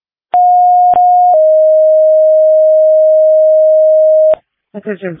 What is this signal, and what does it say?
Two-tone page, heard as a one-second tone, then a three-second tone, about 720 then 620 Hz. It is heard from 0.3 to 4.3 s.